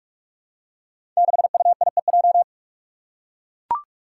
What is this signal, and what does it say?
Morse code, heard as a keyed tone, about 44 wpm, 700 Hz. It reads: 6UIEJ
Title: Five Challenging Letter-Number Combinations 45wpm